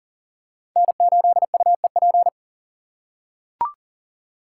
Morse code, heard as a keyed tone, about 39 wpm, 700 Hz. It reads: N8UEP
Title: Code Groups - 5 Letter:Numbers - 40wpm